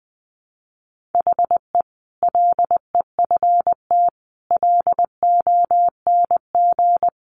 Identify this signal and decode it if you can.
Morse code, heard as a keyed tone, about 20 wpm, 700 Hz. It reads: HE LEFT LONG
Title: Sentences from Top 500 Words 20wpm